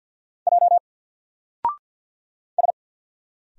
Morse code, heard as a keyed tone, about 50 wpm, 700 Hz. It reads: J S
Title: Single Letter-Number 50wpm